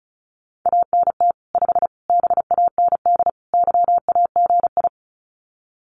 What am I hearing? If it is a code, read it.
Morse code, heard as a keyed tone, about 34 wpm, 700 Hz. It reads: ANT 5 BAND YAGI
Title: Common QSO Phrases 35wpm